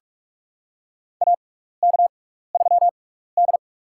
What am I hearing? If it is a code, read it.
Morse code, heard as a keyed tone, about 46 wpm, spaced as for 15 wpm, 700 Hz. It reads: AK3D
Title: USA Call Signs - 4 Characters - 6x Character Spacing - 45wpm